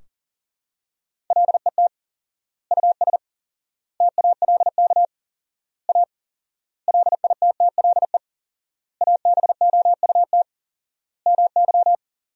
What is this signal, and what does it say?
Morse code, heard as a keyed tone, about 37 wpm, 700 Hz. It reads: LET US TALK A LITTLE ABOUT MY
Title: 4x Word-Spacing - Sentences from Top 400 Words - 40wpm